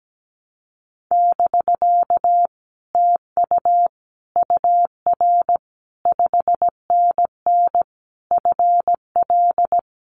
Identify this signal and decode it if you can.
Morse code, heard as a keyed tone, about 17 wpm, 700 Hz. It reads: <BK> TU UR 5NN FL
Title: Common QSO Phrases Encore 17wpm